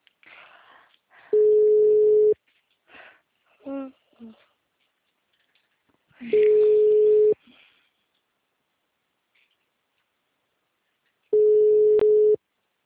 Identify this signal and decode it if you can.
Ringback tone (Europe, 425 Hz), with speech.
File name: out- -2008- - - .54